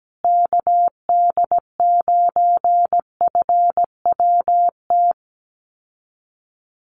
Morse code, heard as a keyed tone, about 17 wpm, 700 Hz. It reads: KD9FWT
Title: USA Call Signs Repeated 2x 17wpm